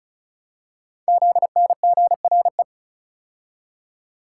Morse code, heard as a keyed tone, about 36 wpm, 690 Hz. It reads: ZNGRE
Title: Code Groups - 5 Letters - 35wpm